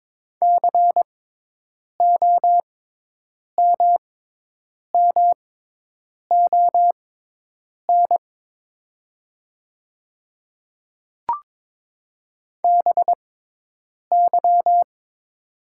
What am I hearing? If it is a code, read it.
Morse code, heard as a keyed tone, about 23 wpm, spaced as for 8 wpm, 700 Hz. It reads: COMMON BY
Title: 6x Character Spacing - Top 500 Words - 22wpm